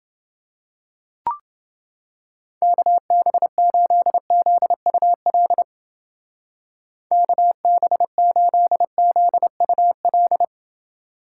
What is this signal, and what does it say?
Morse code, heard as a keyed tone, about 28 wpm, 700 Hz. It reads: KB8ZUL KB8ZUL